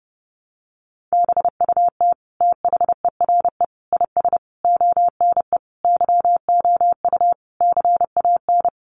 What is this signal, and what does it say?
Morse code, heard as a keyed tone, about 30 wpm, 700 Hz. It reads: BUT THERE IS ONE YOU CAN
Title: Sentences from Top 1200 Words 30wpm